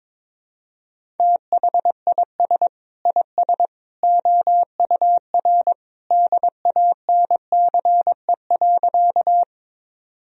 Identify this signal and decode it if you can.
Morse code, heard as a keyed tone, about 22 wpm, 700 Hz. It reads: THIS IS OUR DANCE.
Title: Sentences from Top 1000 Words - Review Part 4 - 22wpm